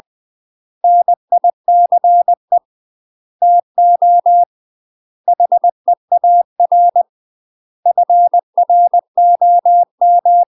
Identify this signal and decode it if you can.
Morse code, heard as a keyed tone, about 20 wpm, 700 Hz. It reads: NICE TO HEAR FROM
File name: UK MASTER QSOs TRAINING / 20WPM / UK MASTER QSOs 0041